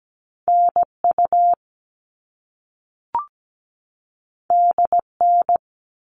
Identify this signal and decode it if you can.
Morse code, heard as a keyed tone, about 17 wpm, 700 Hz. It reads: NU DN
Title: All Cut Numbers - 2-Digit 17wpm